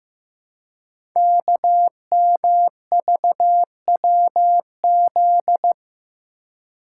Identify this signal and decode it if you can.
Morse code, heard as a keyed tone, about 15 wpm, 690 Hz. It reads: KMVWZ
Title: Code Groups - 5 Letters - 15wpm